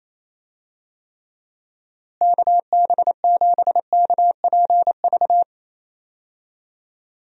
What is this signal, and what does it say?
Morse code, heard as a keyed tone, about 28 wpm, 700 Hz. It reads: KB7KPV
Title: USA Call Signs - 6 Characters - Repeated 3x - 28wpm